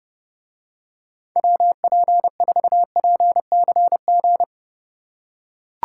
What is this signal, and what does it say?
Morse code, heard as a keyed tone, about 32 wpm, 700 Hz. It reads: WP4PCG